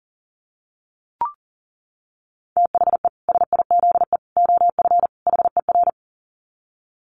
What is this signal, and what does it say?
Morse code, heard as a keyed tone, about 40 wpm, 700 Hz. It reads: THE SIZE OF HER
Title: Dit Challenge 40wpm